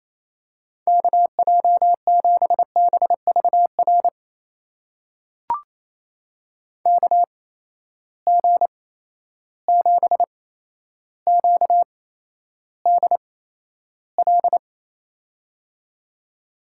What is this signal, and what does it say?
Morse code, heard as a keyed tone, about 28 wpm, spaced as for 12 wpm, 700 Hz. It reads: KJ7BVR KG7QDL
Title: USA Call Signs - 6 Characters - 8x Character Spacing - 28wpm